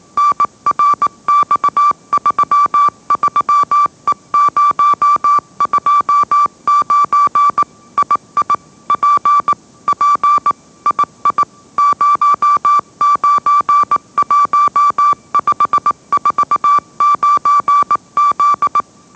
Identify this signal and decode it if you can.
Morse code, heard as a keyed tone, about 18 wpm, 1160 Hz. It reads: NRX33E029 II P P II 091549Z
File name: mrx / mrx081 20wpm